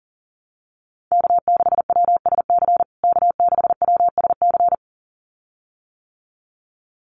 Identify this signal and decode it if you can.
Morse code, heard as a keyed tone, about 40 wpm, 700 Hz. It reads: K6WSC K6WSC
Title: USA Call Signs Repeated 2x 40wpm